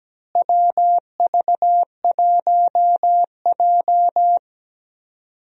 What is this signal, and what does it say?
Morse code, heard as a keyed tone, about 17 wpm, 700 Hz. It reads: WV1J
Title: USA Call Signs - 4 Characters - Repeated 2x - 17wpm